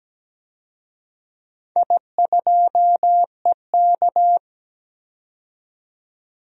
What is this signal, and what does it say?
Morse code, heard as a keyed tone, about 17 wpm, 700 Hz. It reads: I2EK
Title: DXCC-100 Call Signs - 4 Characters - 17wpm